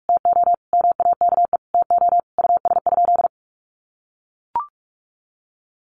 Morse code, heard as a keyed tone, about 44 wpm, 700 Hz. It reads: TO MAKE TO US?
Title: Sentences from Top 1400 Words 45wpm